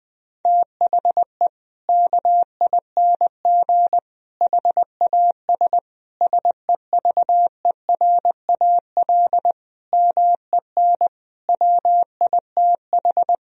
Morse code, heard as a keyed tone, about 20 wpm, 700 Hz. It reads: THE KING HAS SEVERAL MEN WITH
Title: Sentences from Top 1000 Words - Review Part 2 - 20wpm